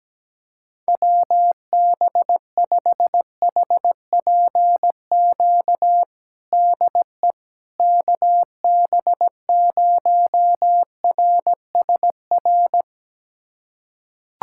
Morse code, heard as a keyed tone, about 17 wpm, 700 Hz. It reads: WB5HPQ DE KB0RSR
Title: USA Callsign DE Callsign - 6 Characters - 17wpm